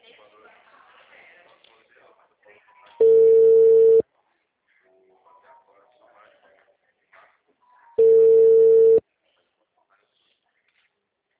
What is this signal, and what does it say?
Ringback tone (China, 450 Hz), with speech.
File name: out- -2022- - - .0